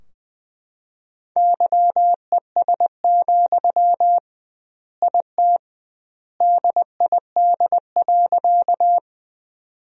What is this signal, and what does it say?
Morse code, heard as a keyed tone, about 20 wpm, 700 Hz. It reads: YES, IT DID.
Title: 2x Word-Spacing - Sentences from Top 500 Words - 20wpm